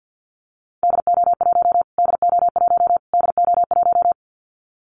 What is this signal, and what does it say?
Morse code, heard as a keyed tone, about 50 wpm, 700 Hz. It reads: DO1 DO1 DO1
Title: Callsign Prefixes for Top 25 DXCC - Repeated 3x - 50wpm